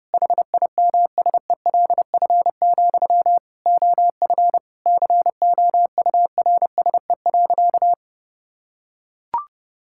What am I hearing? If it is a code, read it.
Morse code, heard as a keyed tone, about 32 wpm, 700 Hz. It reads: HIMSELF, OF COURSE.
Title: Sentences from Top 1600 Words 30wpm